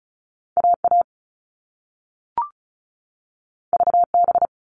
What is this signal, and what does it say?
Morse code, heard as a keyed tone, about 36 wpm, 690 Hz. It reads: AA VB
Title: All Cut Numbers - 2-Digit 35wpm